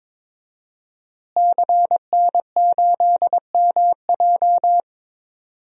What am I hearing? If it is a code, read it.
Morse code, heard as a keyed tone, about 22 wpm, 700 Hz. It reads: CN8MJ
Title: DXCC-100 Call Signs - 5 Characters - Repeated 2x - 22wpm